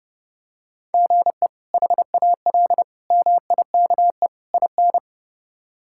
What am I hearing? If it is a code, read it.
Morse code, heard as a keyed tone, about 32 wpm, 700 Hz. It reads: GE HAL MIKE IN
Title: K1USN SST QSO Practice - 30wpm